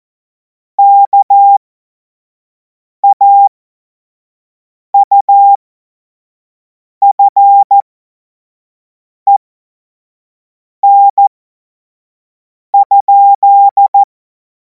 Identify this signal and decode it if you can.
Morse code, heard as a keyed tone, about 14 wpm, spaced as for 5 wpm, 800 Hz. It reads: KAUFEN?
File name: Morsekurs / Witze / Witz 28. April 2026 25 F70